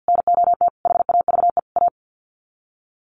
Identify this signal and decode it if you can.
Morse code, heard as a keyed tone, about 50 wpm, 700 Hz. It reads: NOT HAVE A
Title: Sets of 4 Words 50wpm